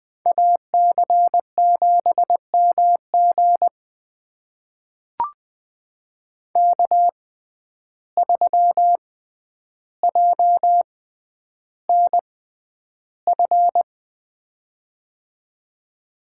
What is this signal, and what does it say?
Morse code, heard as a keyed tone, about 20 wpm, spaced as for 10 wpm, 700 Hz. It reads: AC7MG K3JNF
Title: USA Call Signs - 5 Characters - 6x Character Spacing - 20wpm